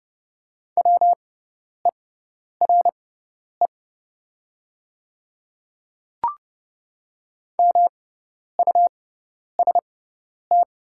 Morse code, heard as a keyed tone, about 30 wpm, spaced as for 10 wpm, 700 Hz. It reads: WERE MUST
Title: 6x Character Spacing - Top 300 Words - 30wpm